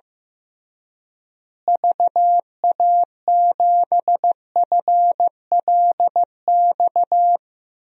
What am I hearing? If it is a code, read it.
Morse code, heard as a keyed tone, about 15 wpm, 700 Hz. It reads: VA7FLX
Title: DXCC-100 Call Signs - 6 Characters - Repeated 2x - 15wpm